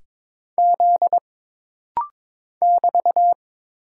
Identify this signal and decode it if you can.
Morse code, heard as a keyed tone, about 22 wpm, 700 Hz. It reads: Z <BT>
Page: This rapid-fire timing is geared to push your ICR (Instant Character Recognition) skills.